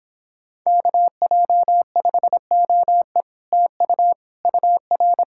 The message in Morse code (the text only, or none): KJ5OE TU UR